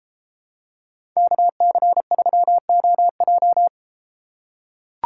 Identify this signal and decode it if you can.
Morse code, heard as a keyed tone, about 31 wpm, 700 Hz. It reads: KC3OJ